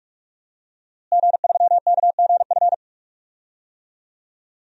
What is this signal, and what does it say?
Morse code, heard as a keyed tone, about 46 wpm, 700 Hz. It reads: G3KGF
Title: DXCC-100 Call Signs - 5 Characters - Repeated 2x - 45wpm